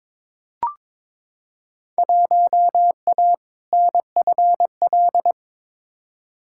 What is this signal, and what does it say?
Morse code, heard as a keyed tone, about 22 wpm, 700 Hz. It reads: 1A NFL
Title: Field Day Exchanges Encore - 22wpm